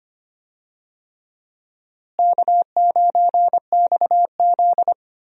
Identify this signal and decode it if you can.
Morse code, heard as a keyed tone, about 25 wpm, 700 Hz. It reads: K9XZ
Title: USA Call Signs - 4 Characters - Repeated 3x - 25wpm